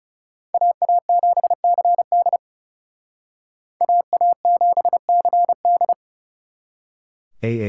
Morse code, heard as a keyed tone, about 39 wpm, 700 Hz. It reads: AA7CD AA7CD